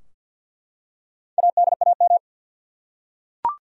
Morse code, heard as a keyed tone, about 52 wpm, 700 Hz. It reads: ADAM
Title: USA Common Names - 50wpm